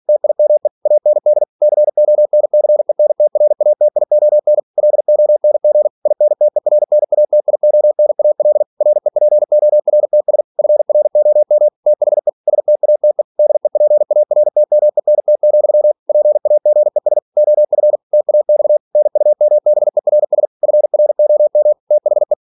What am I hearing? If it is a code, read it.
Morse code, heard as a keyed tone, about 46 wpm, 600 Hz. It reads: TIME AND CONCENTRATION LONG INTERNATIONAL REPORTS FROM THE STATE DEPARTMENT, PAGES OF TAX NUMBERS FROM THE
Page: Never 47wpm